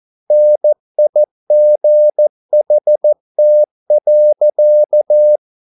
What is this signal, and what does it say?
Morse code, heard as a keyed tone, about 14 wpm, 600 Hz. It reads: NIGHT.